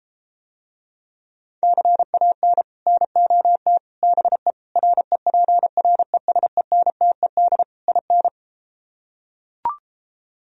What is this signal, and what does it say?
Morse code, heard as a keyed tone, about 33 wpm, 700 Hz. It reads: CAN NOT BE REPRESENTED IN